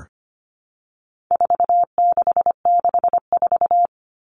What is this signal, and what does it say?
Morse code, heard as a keyed tone, about 25 wpm, 700 Hz. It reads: 4664
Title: Numbers - 4-Digit 25wpm